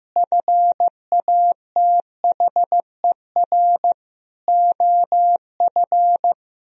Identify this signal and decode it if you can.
Morse code, heard as a keyed tone, about 15 wpm, 700 Hz. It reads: FATHER OF
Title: Sentences from Top 1000 Words - Review Part 2 - 15wpm